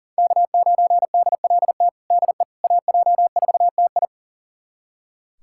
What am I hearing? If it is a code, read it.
Morse code, heard as a keyed tone, about 40 wpm, 700 Hz. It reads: K9DLT DE AJ4TI
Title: USA Callsign DE Callsign - 5 Characters - 40wpm